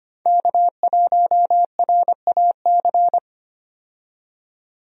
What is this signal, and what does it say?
Morse code, heard as a keyed tone, about 25 wpm, 700 Hz. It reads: K1RAC
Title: USA Call Signs - 5 Characters - 25wpm